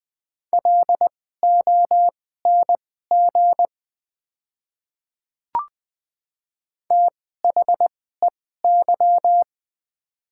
Morse code, heard as a keyed tone, about 20 wpm, spaced as for 14 wpm, 700 Hz. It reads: LONG THEY
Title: 2x Character Spacing - Top 100 Words - 20wpm